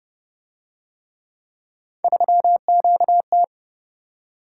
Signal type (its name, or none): Morse code